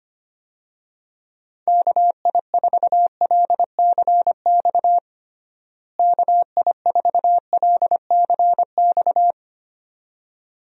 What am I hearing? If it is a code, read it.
Morse code, heard as a keyed tone, about 25 wpm, 700 Hz. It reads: KI4LCX KI4LCX
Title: USA Call Signs - 6 Characters - Repeated 2x - 25wpm